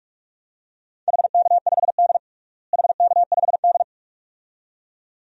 Morse code, heard as a keyed tone, about 46 wpm, 700 Hz. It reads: HK5D HK5D
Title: DXCC-100 Call Signs - 4 Characters - Repeated 2x - 45wpm